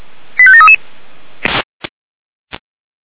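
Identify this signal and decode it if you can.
Five-tone selective call (ZVEI), heard as quick runs of five tones, tones of about 70 ms. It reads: 85733
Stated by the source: ZVEI five-tone call received by SDR.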